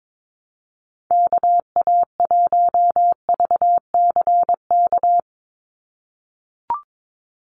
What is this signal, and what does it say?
Morse code, heard as a keyed tone, about 22 wpm, 700 Hz. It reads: KA1VCK
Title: USA Call Signs 22wpm